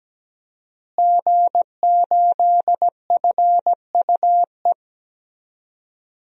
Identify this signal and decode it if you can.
Morse code, heard as a keyed tone, about 17 wpm, 700 Hz. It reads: G8FUE
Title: DXCC-100 Call Signs - 5 Characters - Repeated 3x - 17wpm